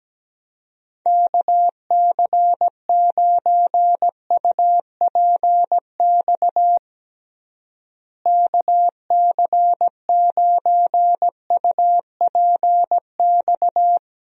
Morse code, heard as a keyed tone, about 17 wpm, 700 Hz. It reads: KC9UPX KC9UPX
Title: USA Call Signs - 6 Characters - Repeated 2x - 17wpm